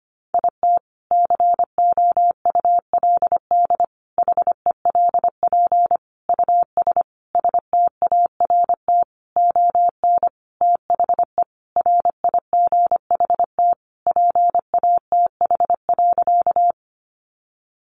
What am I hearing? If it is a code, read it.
Morse code, heard as a keyed tone, about 25 wpm, 700 Hz. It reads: IT COULD HELP US START ON THE RIGHT PATH.
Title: Sentences from Top 1400 Words 25wpm